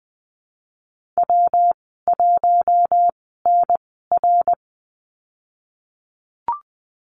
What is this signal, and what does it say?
Morse code, heard as a keyed tone, about 20 wpm, spaced as for 15 wpm, 700 Hz. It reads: W1NR
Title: 167 Lesson 27g - Number 7 - Call Signs 20wpm